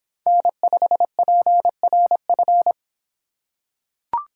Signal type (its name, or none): Morse code